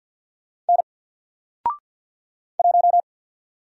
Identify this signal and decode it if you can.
Morse code, heard as a keyed tone, about 50 wpm, 700 Hz. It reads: N 1
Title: Single Letter-Number 50wpm